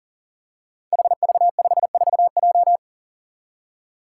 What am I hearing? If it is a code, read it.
Morse code, heard as a keyed tone, about 40 wpm, 700 Hz. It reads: HV54J
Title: Five Challenging Letter-Number Combinations 40wpm